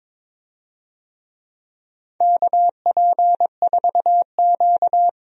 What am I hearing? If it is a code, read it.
Morse code, heard as a keyed tone, about 22 wpm, 700 Hz. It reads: KP4Q
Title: DXCC-100 Call Signs - 4 Characters - 22wpm